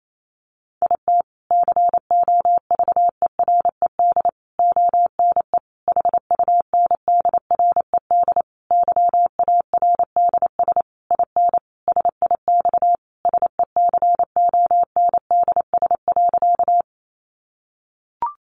Morse code, heard as a keyed tone, about 28 wpm, 700 Hz. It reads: IT COVERED ONE HUNDRED YARDS IN SIX SECONDS.
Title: A Sound Of Thunder 28wpm